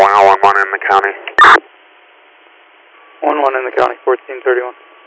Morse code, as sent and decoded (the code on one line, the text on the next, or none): none